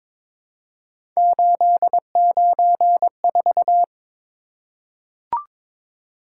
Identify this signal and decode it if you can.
Morse code, heard as a keyed tone, about 22 wpm, 700 Hz. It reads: 894
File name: practice / 22wpm / Three Letter-Number Combinations - 22wpm